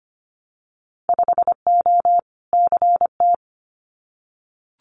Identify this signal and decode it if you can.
Morse code, heard as a keyed tone, about 25 wpm, 700 Hz. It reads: 5O CT
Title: Winter Field Day Exchanges Encore 25wpm